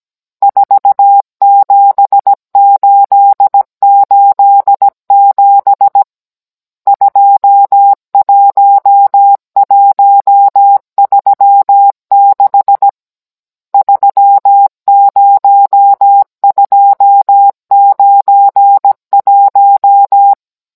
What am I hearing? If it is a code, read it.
Morse code, heard as a keyed tone, about 17 wpm, 800 Hz. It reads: 47887 21136 30291
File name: Cours-NR19-16mm